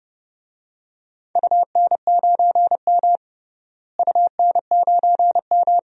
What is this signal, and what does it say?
Morse code, heard as a keyed tone, about 29 wpm, 700 Hz. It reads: UN9M UN9M
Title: DXCC-100 Call Signs - 4 Characters - Repeated 2x - 30wpm